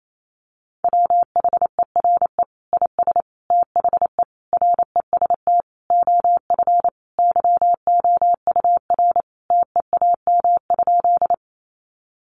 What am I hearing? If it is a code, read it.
Morse code, heard as a keyed tone, about 28 wpm, 700 Hz. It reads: WHERE IS THE REST OF YOUR TEAM?
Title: Sentences from Top 1500 Words 28wpm